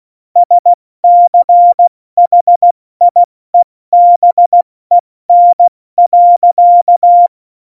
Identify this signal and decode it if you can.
Morse code, heard as a keyed tone, about 16 wpm, 700 Hz. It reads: SCHIEBEN.